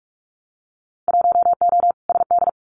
Morse code, heard as a keyed tone, about 44 wpm, 700 Hz. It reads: 1O SD
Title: Winter Field Day Exchanges Encore 45wpm